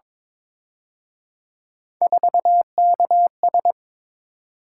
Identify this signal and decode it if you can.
Morse code, heard as a keyed tone, about 26 wpm, 700 Hz. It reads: 4KS
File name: Speed-Racing - Three Letter-Number Combinations - 22wpm